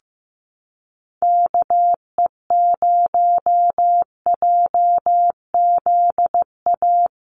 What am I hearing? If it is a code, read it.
Morse code, heard as a keyed tone, about 15 wpm, 700 Hz. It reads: KE0JZA
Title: USA Call Signs Repeated 2x 15wpm